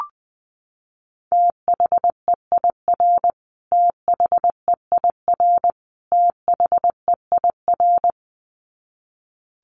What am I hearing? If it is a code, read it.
Morse code, heard as a keyed tone, about 20 wpm, 700 Hz. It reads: THEIR THEIR THEIR
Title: Top 500 Words Repeated 20wpm